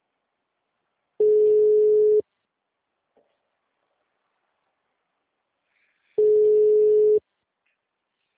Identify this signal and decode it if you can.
Ringback tone (Europe, 425 Hz).